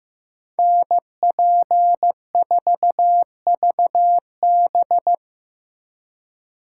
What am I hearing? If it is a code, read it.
Morse code, heard as a keyed tone, about 18 wpm, 700 Hz. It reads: NP4VB
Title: Speed Racing - USA 2x2 Call Signs 15wpm